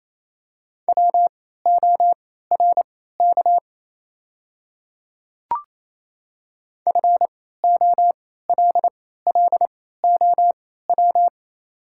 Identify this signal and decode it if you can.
Morse code, heard as a keyed tone, about 29 wpm, spaced as for 16 wpm, 700 Hz. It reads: WORK FOLLOW
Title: 3x Character Spacing - Top 300 Words - 28wpm